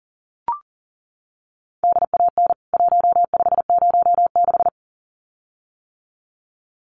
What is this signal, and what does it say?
Morse code, heard as a keyed tone, about 40 wpm, 700 Hz. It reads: DAN 1506
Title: CWT Exchange 40wpm